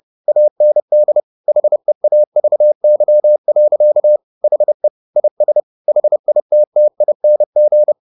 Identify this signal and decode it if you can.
Morse code, heard as a keyed tone, about 30 wpm, 600 Hz. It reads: AND HEAVY. HE IS HITTING